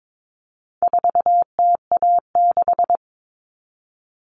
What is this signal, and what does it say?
Morse code, heard as a keyed tone, about 22 wpm, 700 Hz. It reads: 4TA6
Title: All Cut Numbers - 4-Digit 22wpm